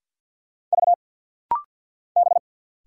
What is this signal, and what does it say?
Morse code, heard as a keyed tone, about 50 wpm, 690 Hz. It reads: V B
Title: Single Letter-Number Alt - Rapid-Fire 50wpm